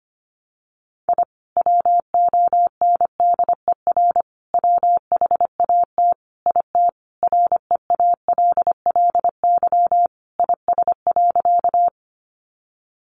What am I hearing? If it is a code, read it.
Morse code, heard as a keyed tone, about 25 wpm, 700 Hz. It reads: I WONDER WHAT IT REALLY IS.
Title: Sentences from Top 600 Words 25wpm